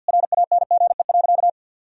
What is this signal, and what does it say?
Morse code, heard as a keyed tone, about 50 wpm, 700 Hz. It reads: RANGE.
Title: Sentences from Top 1000 Words - Review Part 4 - 50wpm